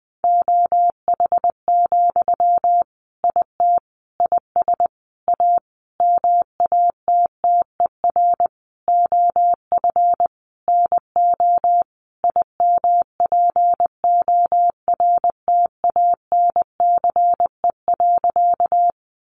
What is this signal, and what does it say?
Morse code, heard as a keyed tone, about 20 wpm, 700 Hz. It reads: OH, IT IS A MATTER OF NO IMPORTANCE.
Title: Sentences from Top 1200 Words 20wpm